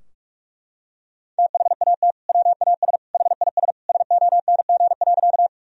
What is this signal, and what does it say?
Morse code, heard as a keyed tone, about 41 wpm, 700 Hz. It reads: THAT WAS HIS SONG.
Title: Sentences from Top 1000 Words - Review Part 2 - 45wpm